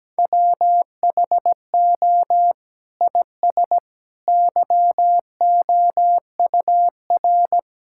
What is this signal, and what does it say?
Morse code, heard as a keyed tone, about 17 wpm, 700 Hz. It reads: WHO IS YOUR
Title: Sentences from Top 1600 Words 17wpm